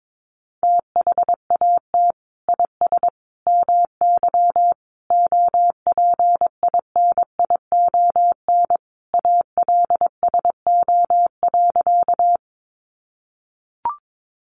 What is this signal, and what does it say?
Morse code, heard as a keyed tone, about 22 wpm, 700 Hz. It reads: THAT IS MY OPINION ALSO.
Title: Sentences from Top 1200 Words 22wpm